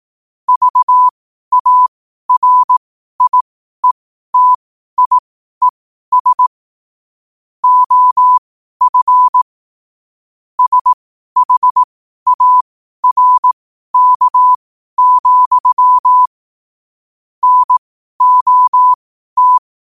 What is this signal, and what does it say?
Morse code, heard as a keyed tone, about 18 wpm, spaced as for 12 wpm, 1000 Hz. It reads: VARIETIES OF SHARK, NOT
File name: qotdfiles / 12wpm